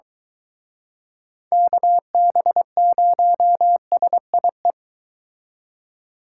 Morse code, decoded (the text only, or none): KB0SIE